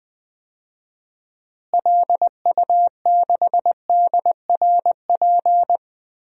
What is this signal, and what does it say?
Morse code, heard as a keyed tone, about 21 wpm, 700 Hz. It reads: LU6DRP
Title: DXCC-100 Call Signs - 6 Characters - Repeated 3x - 20wpm